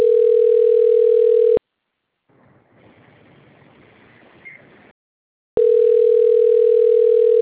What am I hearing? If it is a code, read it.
Ringback tone (North America), 440 + 480 Hz.